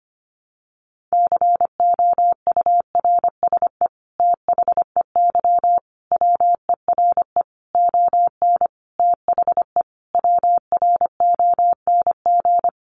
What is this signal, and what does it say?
Morse code, heard as a keyed tone, about 25 wpm, 700 Hz. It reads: COURSE THEY WERE ON THE WRONG
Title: Sentences from Top 900 Words 25wpm